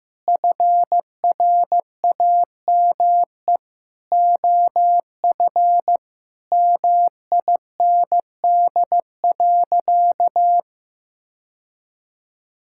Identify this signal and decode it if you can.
Morse code, heard as a keyed tone, about 15 wpm, 700 Hz. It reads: FRAME OF MIND.
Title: Sentences from Top 1000 Words - Review Part 6 - 15wpm